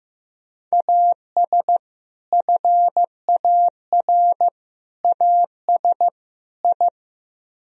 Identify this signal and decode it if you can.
Morse code, heard as a keyed tone, about 16 wpm, 700 Hz. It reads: AS FAR AS I
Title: Speed-Racing - Sets of 4 Words 15wpm